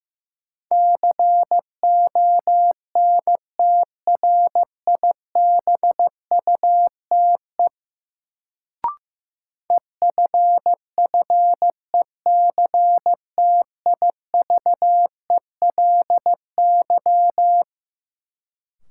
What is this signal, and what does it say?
Morse code, heard as a keyed tone, about 16 wpm, 700 Hz. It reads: CONTRIBUTE EFFECTIVELY
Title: Sentences from Top 1400 Words - New Word Drill Alt - 15wpm